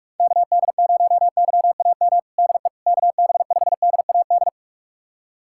Morse code, heard as a keyed tone, about 44 wpm, 700 Hz. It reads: KD0YAM DE KB5DAD
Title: USA Callsign DE Callsign - 6 Characters - 45wpm